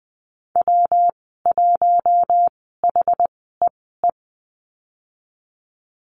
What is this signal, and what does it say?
Morse code, heard as a keyed tone, about 20 wpm, spaced as for 15 wpm, 700 Hz. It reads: W1HEE
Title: 128 Lesson 21g - Number 6 - Call Signs 20wpm